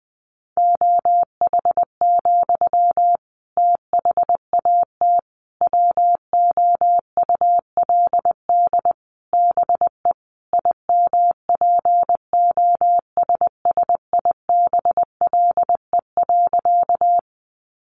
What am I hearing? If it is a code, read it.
Morse code, heard as a keyed tone, about 20 wpm, 700 Hz. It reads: OH, THAT WOULD BE IMPOSSIBLE.
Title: Sentences from Top 1100 Words 20wpm